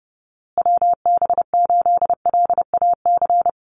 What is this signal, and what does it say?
Morse code, heard as a keyed tone, about 30 wpm, 700 Hz. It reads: WB8LAC
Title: USA Call Signs Repeated 2x 30wpm